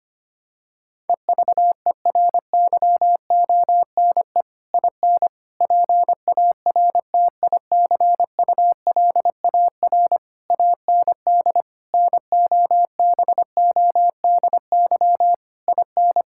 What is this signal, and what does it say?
Morse code, heard as a keyed tone, about 25 wpm, 700 Hz. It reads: EVERYONE IN PARTICULAR AND NOBODY IN
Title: Sentences from Top 1600 Words 25wpm